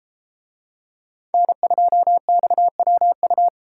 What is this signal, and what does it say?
Morse code, heard as a keyed tone, about 32 wpm, 700 Hz. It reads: N2XWU